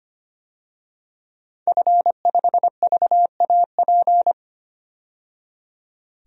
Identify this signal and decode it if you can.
Morse code, heard as a keyed tone, about 25 wpm, 700 Hz. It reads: F5VAP
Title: DXCC-100 Call Signs - 5 Characters - 25wpm